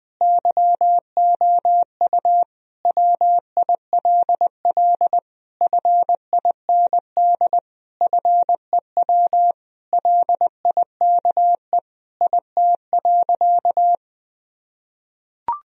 Morse code, heard as a keyed tone, about 21 wpm, 700 Hz. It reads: YOU WILL FIND FEW LIKE IT.
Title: Sentences from Top 1000 Words - Review Part 1 - 20wpm